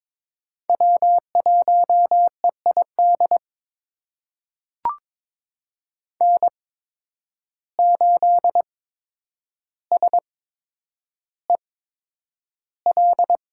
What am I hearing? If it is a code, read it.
Morse code, heard as a keyed tone, about 22 wpm, spaced as for 9 wpm, 700 Hz. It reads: W1EID N8SEL
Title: USA Call Signs - 8x Character Spacing - 22wpm